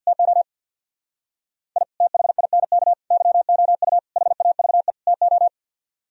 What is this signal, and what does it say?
Morse code, heard as a keyed tone, about 62 wpm, 700 Hz. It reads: TO I THINK YOU HAVE TO
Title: Speed-Racing - Sets of 5 Words 50wpm